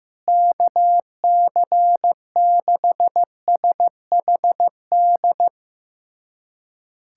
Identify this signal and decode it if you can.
Morse code, heard as a keyed tone, about 15 wpm, 700 Hz. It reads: KC6SHD
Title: USA Callsign DE Callsign 15wpm